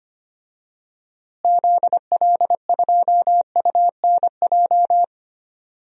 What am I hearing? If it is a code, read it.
Morse code, heard as a keyed tone, about 25 wpm, 700 Hz. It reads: ZL2UNJ
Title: DXCC-100 Call Signs - 6 Characters - Repeated 3x - 25wpm